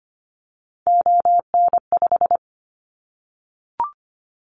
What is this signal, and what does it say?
Morse code, heard as a keyed tone, about 25 wpm, 700 Hz. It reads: ON5
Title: Callsign Prefixes for Top 25 DXCC - 25wpm